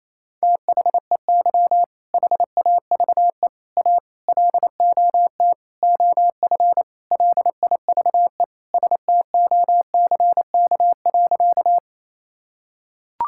Morse code, heard as a keyed tone, about 28 wpm, 700 Hz. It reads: THEY HAVE A LOT OF LIVE STOCK.
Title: Sentences from Top 1000 Words - Review Part 5 - 28wpm